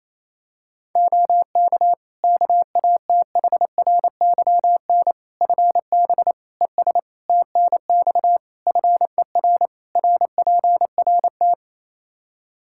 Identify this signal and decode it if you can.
Morse code, heard as a keyed tone, about 28 wpm, 700 Hz. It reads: OK KATHRYN FB ES TNX FER RPRT
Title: Practice QSO - Exchange 2 - Info - 28wpm